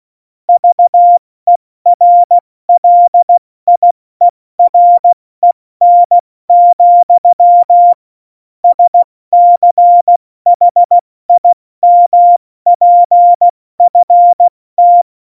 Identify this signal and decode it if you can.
Morse code, heard as a keyed tone, about 16 wpm, 700 Hz. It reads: VERLIEREN, SCHIMPFT
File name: Morsekurs / Witze / Witz 25. Februar 2020, 70 F80